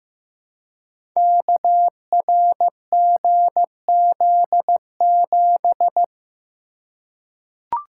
Morse code, heard as a keyed tone, about 15 wpm, 700 Hz. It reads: KRGZ7
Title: Code Groups - 5 Letter:Numbers - 15wpm